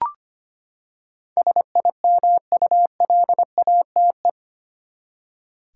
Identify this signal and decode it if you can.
Morse code, heard as a keyed tone, about 25 wpm, 700 Hz. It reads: SIMULATE
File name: Eight Letter Words - 25wpm